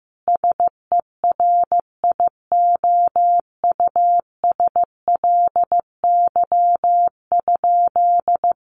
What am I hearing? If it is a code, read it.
Morse code, heard as a keyed tone, about 15 wpm, 700 Hz. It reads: SERIOUSLY?
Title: Sentences from Top 1200 Words 15wpm